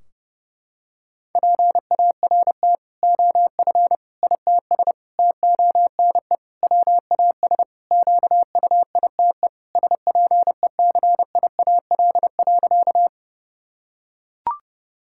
Morse code, heard as a keyed tone, about 32 wpm, 700 Hz. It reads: PART OF ITS TONE WAS QUITE SPECIAL.
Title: Sentences from Top 1600 Words 30wpm